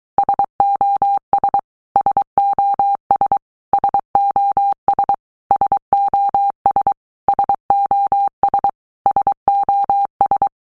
Morse code, heard as a keyed tone, about 23 wpm, 800 Hz. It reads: SOS SOS SOS SOS SOS SOS
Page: SOS sound in Morse Code (Fast)